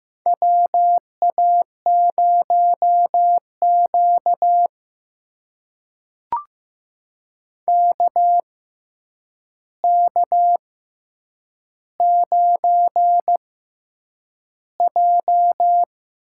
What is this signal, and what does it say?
Morse code, heard as a keyed tone, about 15 wpm, spaced as for 8 wpm, 700 Hz. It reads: WA0Q KK9J
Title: USA Call Signs - 4 Characters - 6x Character Spacing - 15wpm